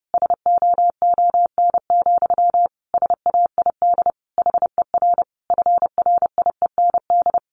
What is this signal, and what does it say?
Morse code, heard as a keyed tone, about 30 wpm, 700 Hz. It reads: SOON, SAID HER FRIEND
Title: Sentences from Top 1500 Words 30wpm